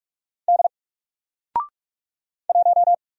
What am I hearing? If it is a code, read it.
Morse code, heard as a keyed tone, about 44 wpm, 700 Hz. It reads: D 1
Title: Single Character (Letter:Number:Punctuation:Prosign) 45wpm